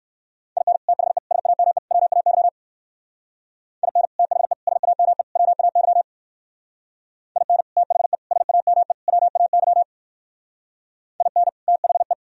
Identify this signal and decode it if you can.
Morse code, heard as a keyed tone, about 58 wpm, 700 Hz. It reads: IN THE SAME WAY IN THE SAME WAY IN THE SAME WAY IN THE